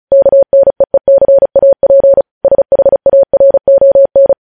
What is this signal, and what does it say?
Morse code, heard as a keyed tone, about 36 wpm, 550 Hz. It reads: KNEECAP SHARON
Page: news headlines in morse code at 35 wpm